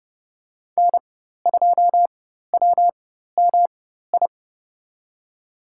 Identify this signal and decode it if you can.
Morse code, heard as a keyed tone, about 31 wpm, spaced as for 14 wpm, 700 Hz. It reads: N2WMI
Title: USA Call Signs - 5 Characters - 4x Character Spacing - 30wpm